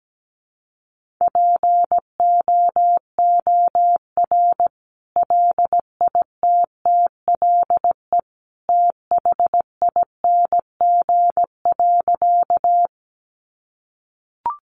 Morse code, heard as a keyed tone, about 17 wpm, 700 Hz. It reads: POOR LITTLE THING.
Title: Sentences from Top 1000 Words 17wpm